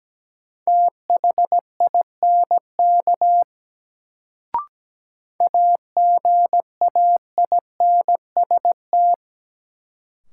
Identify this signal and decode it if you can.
Morse code, heard as a keyed tone, about 17 wpm, 700 Hz. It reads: THINK AGAINST
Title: Sets of 2 Words - New Word Drill - 17wpm